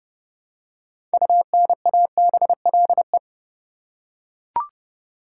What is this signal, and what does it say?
Morse code, heard as a keyed tone, about 30 wpm, 700 Hz. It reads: UNABLE
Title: Six Letter Words - 30wpm